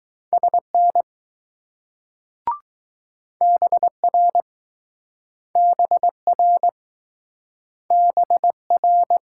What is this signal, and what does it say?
Morse code, heard as a keyed tone, about 20 wpm, 700 Hz. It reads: SN BR BR BR